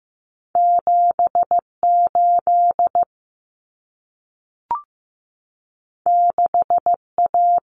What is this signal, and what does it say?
Morse code, heard as a keyed tone, about 15 wpm, 700 Hz. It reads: 78 6A
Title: Common Cut Numbers - 2-Digit 15wpm